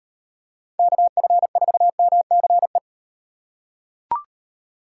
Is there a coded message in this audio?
Yes: Morse code from 0.8 to 2.8 s, heard as a keyed tone, about 36 wpm, 700 Hz. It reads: KF4MCE